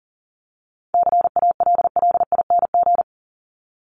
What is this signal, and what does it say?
Morse code, heard as a keyed tone, about 40 wpm, 700 Hz. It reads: CALLING
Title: ING Suffix - Top 100 Words with Base Words 40wpm